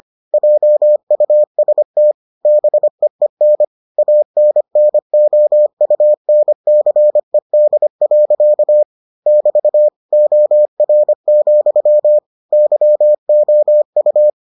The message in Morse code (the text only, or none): JUST BEEN ANNOUNCED. <BT> OR, YOU